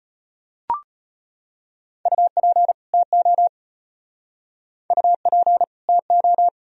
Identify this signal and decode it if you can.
Morse code, heard as a keyed tone, about 33 wpm, 700 Hz. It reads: UP TO UP TO